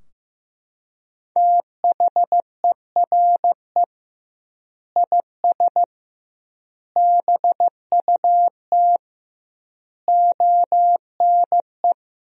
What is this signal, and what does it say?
Morse code, heard as a keyed tone, about 15 wpm, 700 Hz. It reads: THERE IS BUT ONE
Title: 2x Word-Spacing - Sentences from Top 100 Words - 15wpm